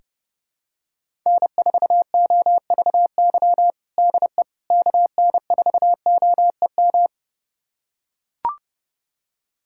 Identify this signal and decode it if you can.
Morse code, heard as a keyed tone, about 30 wpm, 700 Hz. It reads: N4OVY DE KN4OEM
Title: USA Callsign DE Callsign 30wpm